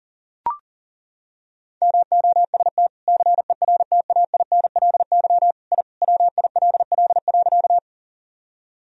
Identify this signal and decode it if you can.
Morse code, heard as a keyed tone, about 40 wpm, 700 Hz. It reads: MOST CERTAINLY I WILL.
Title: Sentences from Top 1000 Words - Review Part 1 - 40wpm